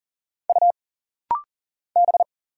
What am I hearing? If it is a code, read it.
Morse code, heard as a keyed tone, about 40 wpm, 700 Hz. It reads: U B
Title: Single Letters - Rapid-Fire 40wpm